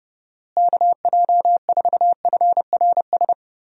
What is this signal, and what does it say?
Morse code, heard as a keyed tone, about 31 wpm, 700 Hz. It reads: KJ4FRS
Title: USA Call Signs - 6 Characters - Repeated 3x - 30wpm